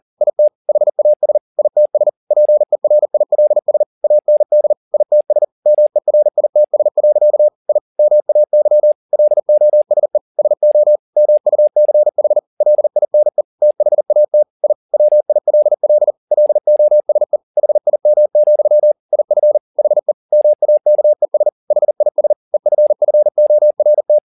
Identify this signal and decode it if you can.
Morse code, heard as a keyed tone, about 40 wpm, 600 Hz. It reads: IT HAS ITS PERILS AND ITS MERITS. I MAY LOSE SO MUCH LINE THAT I WILL LOSE HIM, IF HE MAKES HIS EFFORT
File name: the old man and the sea 2 40wpm